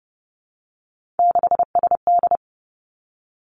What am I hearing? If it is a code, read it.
Morse code, heard as a keyed tone, about 30 wpm, 700 Hz. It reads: 6SD
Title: Three Challenging Letter-Number Combinations 30wpm